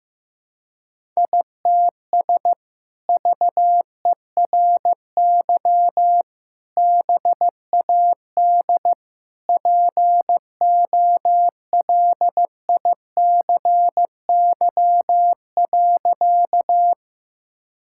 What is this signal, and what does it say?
Morse code, heard as a keyed tone, about 15 wpm, 700 Hz. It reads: ITS VERY BAD POLICY.
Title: Sentences from Top 1000 Words - Review Part 5 - 15wpm